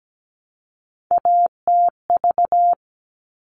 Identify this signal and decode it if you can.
Morse code, heard as a keyed tone, about 17 wpm, 700 Hz. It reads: ATV
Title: All Cut Numbers - 3-Digit 17wpm